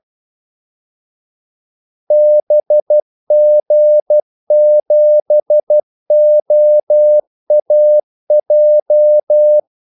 Morse code, heard as a keyed tone, about 12 wpm, 600 Hz. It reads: BG7OAJ
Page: 2 ekstra ordmellemrum mellem kaldesignalerne